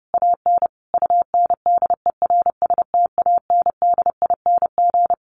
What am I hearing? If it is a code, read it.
Morse code, heard as a keyed tone, about 30 wpm, 700 Hz. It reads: AN UNDERSTANDING
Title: Sentences from Top 1500 Words 30wpm